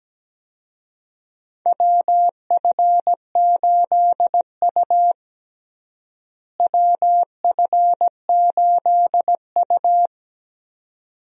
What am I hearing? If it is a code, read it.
Morse code, heard as a keyed tone, about 17 wpm, 700 Hz. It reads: WF8U WF8U
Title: USA Call Signs - 4 Characters - Repeated 2x - 17wpm